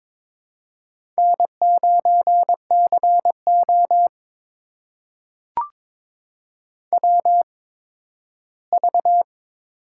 Morse code, heard as a keyed tone, about 22 wpm, spaced as for 13 wpm, 700 Hz. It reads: N9CO WV
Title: USA Call Signs - 4 Characters - 8x Character Spacing - 22wpm